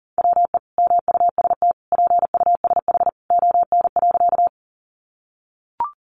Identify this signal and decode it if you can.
Morse code, heard as a keyed tone, about 40 wpm, 700 Hz. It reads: WE MUST PUSH ON.
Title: Sentences from Top 1300 Words 40wpm